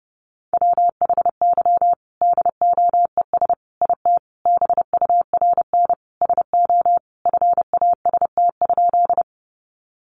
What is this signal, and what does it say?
Morse code, heard as a keyed tone, about 30 wpm, 700 Hz. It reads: WHY DOES IT BURN SO FAST?
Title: Sentences from Top 1300 Words 30wpm